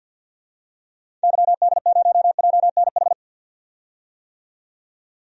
Morse code, heard as a keyed tone, about 46 wpm, 700 Hz. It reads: YD0JNH
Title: DXCC-100 Call Signs - 6 Characters - 50wpm